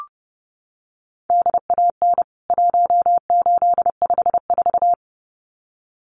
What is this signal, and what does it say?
Morse code, heard as a keyed tone, about 30 wpm, 700 Hz. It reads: DAN 1854
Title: CWT Exchange 30wpm